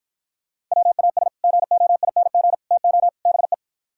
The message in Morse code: WAS GOING TO BE

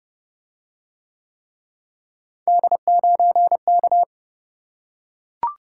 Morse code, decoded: D9K

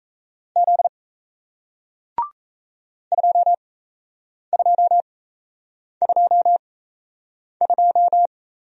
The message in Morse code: Z 2222

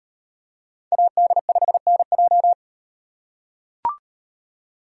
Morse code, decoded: AD5NJ

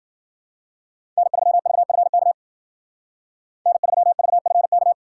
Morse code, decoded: N3VFK N3VFK

